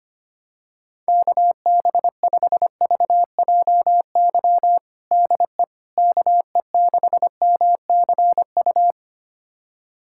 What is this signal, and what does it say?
Morse code, heard as a keyed tone, about 25 wpm, 700 Hz. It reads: KB5VJY DE KE6MCU
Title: USA Callsign DE Callsign 25wpm